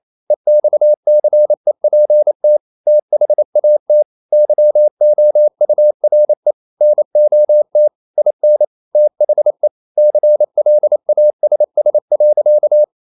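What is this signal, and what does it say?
Morse code, heard as a keyed tone, about 28 wpm, 600 Hz. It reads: EXCEPT THAT YOURE NOT IN THE CLASS.
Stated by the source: Morse code practice at 28 wpm, text of Ken Follett's novel Never in English, from CW med Gnister.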